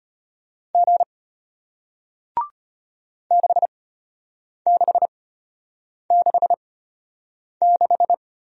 Morse code, heard as a keyed tone, about 33 wpm, spaced as for 8 wpm, 690 Hz. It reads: G 6666